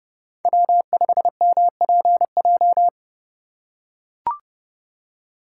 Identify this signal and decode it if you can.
Morse code, heard as a keyed tone, about 29 wpm, 700 Hz. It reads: W5MPJ